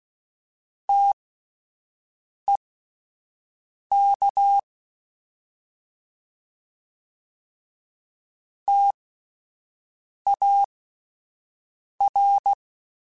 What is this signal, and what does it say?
Morse code, heard as a keyed tone, about 16 wpm, spaced as for 5 wpm, 800 Hz. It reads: TEK TAR